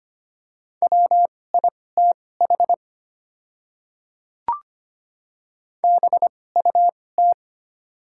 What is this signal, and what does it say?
Morse code, heard as a keyed tone, about 25 wpm, spaced as for 18 wpm, 690 Hz. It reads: WITH BUT
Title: 2x Character Spacing - Top 200 Words - 25wpm